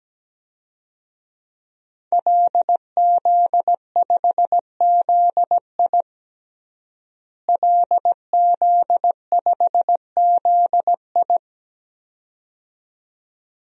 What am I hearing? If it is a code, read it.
Morse code, heard as a keyed tone, about 17 wpm, 700 Hz. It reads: LZ5ZI LZ5ZI
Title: DXCC-100 Call Signs - 5 Characters - Repeated 2x - 17wpm